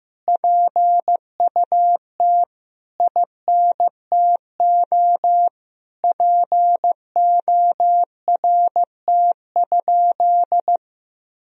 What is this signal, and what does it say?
Morse code, heard as a keyed tone, about 16 wpm, 700 Hz. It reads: PUT INTO PORT?
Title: Sentences from Top 1000 Words - Review Part 1 - 15wpm